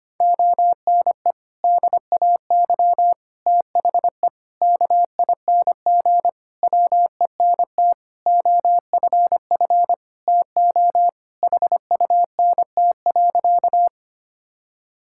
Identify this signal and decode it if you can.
Morse code, heard as a keyed tone, about 25 wpm, 700 Hz. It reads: ONE DAY THE KING WENT OFF TO HUNT.
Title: Sentences from Top 1600 Words 25wpm